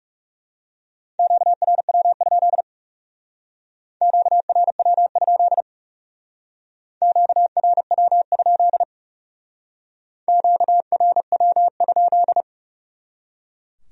Morse code, heard as a keyed tone, about 38 wpm, 700 Hz. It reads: QRW? QRW? QRW? QRW?